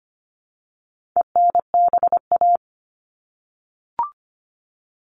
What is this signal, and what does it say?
Morse code, heard as a keyed tone, about 25 wpm, 700 Hz. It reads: ENBA
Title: All Cut Numbers - 4-Digit 25wpm